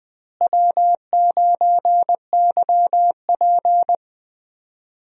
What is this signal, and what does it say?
Morse code, heard as a keyed tone, about 20 wpm, 700 Hz. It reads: W9YP
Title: USA Call Signs - 4 Characters - Repeated 3x - 20wpm